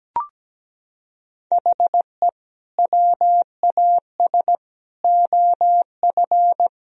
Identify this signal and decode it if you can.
Morse code, heard as a keyed tone, about 17 wpm, 700 Hz. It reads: HE WAS OF
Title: Sentences from Top 1000 Words - Review Part 1 - 17wpm